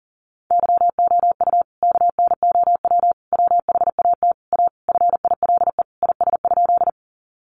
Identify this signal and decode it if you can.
Morse code, heard as a keyed tone, about 40 wpm, 700 Hz. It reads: YOU KNOW WHAT A FILE IS?
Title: Sentences from Top 1200 Words 40wpm